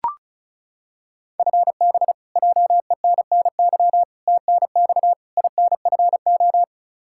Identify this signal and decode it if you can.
Morse code, heard as a keyed tone, about 36 wpm, 700 Hz. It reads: FB JENNY TNX INFO
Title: K1USN SST QSO Practice 35wpm